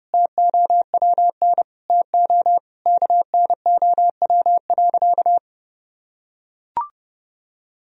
Morse code, heard as a keyed tone, about 30 wpm, 700 Hz. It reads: TOWN TO KNOW.
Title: Sentences from Top 1000 Words - Review Part 2 - 30wpm